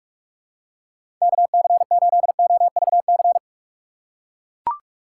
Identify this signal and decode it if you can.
Morse code, heard as a keyed tone, about 42 wpm, 700 Hz. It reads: KC8OVC
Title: USA Call Signs - 6 Characters - 6x Character Spacing - 45wpm